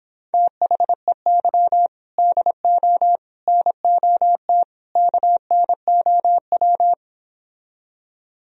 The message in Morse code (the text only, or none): THEY DO NOT KNOW